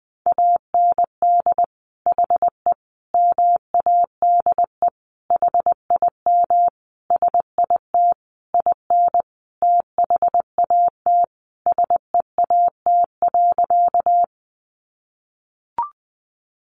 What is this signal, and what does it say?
Morse code, heard as a keyed tone, about 20 wpm, 700 Hz. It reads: AND HE MADE HIM SIT IN THAT SEAT.
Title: Sentences from Top 1500 Words 20wpm